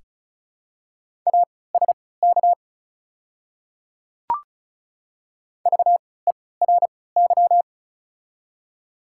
Morse code, heard as a keyed tone, about 34 wpm, spaced as for 20 wpm, 700 Hz. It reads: ASK VERY
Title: 3x Character Spacing - Top 300 Words - 35wpm